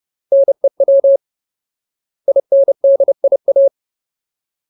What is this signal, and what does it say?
Morse code, heard as a keyed tone, about 30 wpm, 550 Hz. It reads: NEW INDIA
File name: Ultimate Word List / 30WPM Ultimate